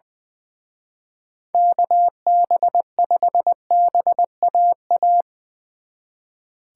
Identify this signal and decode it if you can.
Morse code, heard as a keyed tone, about 22 wpm, 700 Hz. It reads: KB5BAA
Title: Speed Racing - USA 2x3 Call Signs 20wpm